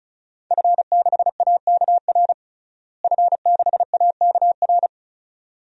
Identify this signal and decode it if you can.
Morse code, heard as a keyed tone, about 36 wpm, 700 Hz. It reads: F6AKR F6AKR
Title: DXCC-100 Call Signs - 5 Characters - Repeated 2x - 35wpm